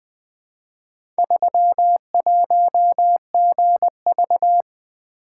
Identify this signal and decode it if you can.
Morse code, heard as a keyed tone, about 20 wpm, 700 Hz. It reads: 31GV
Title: Four Letter-Number Combinations - 20wpm